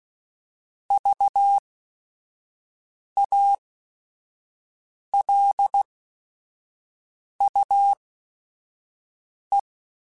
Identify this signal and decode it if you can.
Morse code, heard as a keyed tone, about 16 wpm, spaced as for 5 wpm, 800 Hz. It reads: VALUE
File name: Technical Subjects / Technical Subjects-mp3s / XC @5